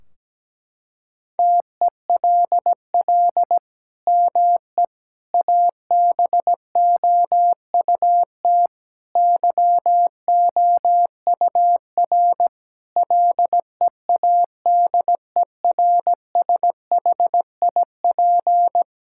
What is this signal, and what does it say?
Morse code, heard as a keyed tone, about 17 wpm, 700 Hz. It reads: TELL ME ABOUT YOUR LEADERSHIP
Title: Sentences from Top 1600 Words 17wpm